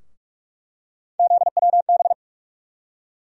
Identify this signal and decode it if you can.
Morse code, heard as a keyed tone, about 46 wpm, 700 Hz. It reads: ZWB
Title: USA Call Sign Postfixes - 3 Letters 45wpm